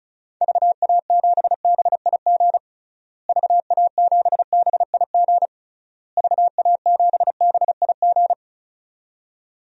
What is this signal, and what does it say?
Morse code, heard as a keyed tone, about 36 wpm, 700 Hz. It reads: VA7BIG VA7BIG VA7BIG
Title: DXCC-100 Call Signs - 6 Characters - Repeated 3x - 35wpm